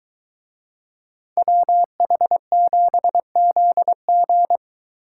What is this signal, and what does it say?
Morse code, heard as a keyed tone, about 22 wpm, 700 Hz. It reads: WH7ZG